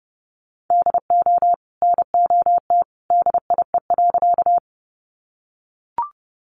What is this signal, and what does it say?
Morse code, heard as a keyed tone, about 30 wpm, 700 Hz. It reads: DO NOT DIE.
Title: Sentences from Top 1300 Words 30wpm